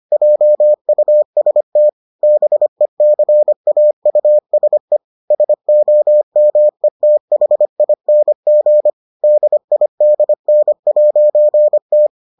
Morse code, heard as a keyed tone, about 25 wpm, 600 Hz. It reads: JUST BECAUSE SOMETHING DIDN'T